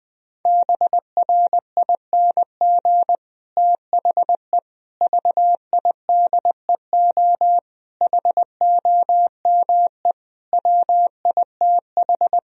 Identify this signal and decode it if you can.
Morse code, heard as a keyed tone, about 21 wpm, 700 Hz. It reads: BRING THE VIDEO HOME WITH
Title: Sentences from Top 1000 Words - Review Part 4 - 20wpm